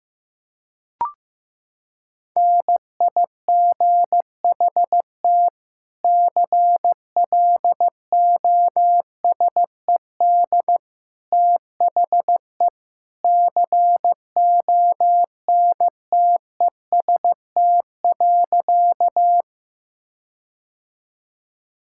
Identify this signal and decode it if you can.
Morse code, heard as a keyed tone, about 15 wpm, 700 Hz. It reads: NIGHT CLOSED THE CONTEST.
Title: Sentences from Top 1600 Words 15wpm